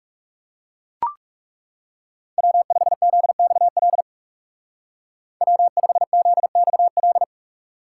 Morse code, heard as a keyed tone, about 37 wpm, 700 Hz. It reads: W5ZXL W5ZXL
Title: Speed Racing - USA 1x3 Call Signs 30wpm